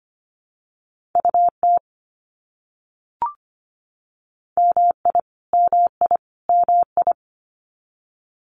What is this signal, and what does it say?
Morse code, heard as a keyed tone, about 25 wpm, 700 Hz. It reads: UT MI MI MI
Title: US States Repeated 3x 25wpm